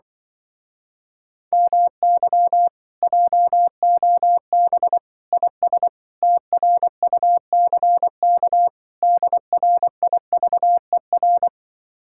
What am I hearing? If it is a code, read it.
Morse code, heard as a keyed tone, about 23 wpm, 700 Hz. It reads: MY JOB IS TRUCK DRIVER
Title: Speed-Racing Common QSO Phrases - 20wpm